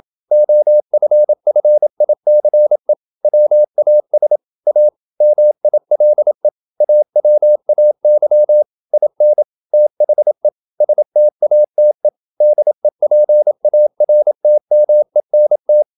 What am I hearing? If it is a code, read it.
Morse code, heard as a keyed tone, about 27 wpm, 600 Hz. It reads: OFFICE WAS A MILE AWAY IN THE STATE DEPARTMENT
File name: Never 0027wpm